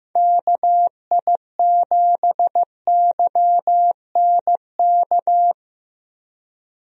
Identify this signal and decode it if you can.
Morse code, heard as a keyed tone, about 15 wpm, 700 Hz. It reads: KI7YNK
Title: USA Call Signs - 6 Characters - Repeated 3x - 15wpm